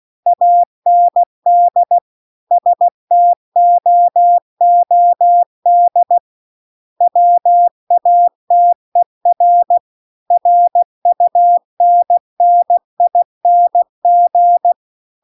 Morse code, heard as a keyed tone, about 16 wpm, 700 Hz. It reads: AND STOOD WATER RUNNING
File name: CW-The-Big-Twoheaded-River-2-16WPM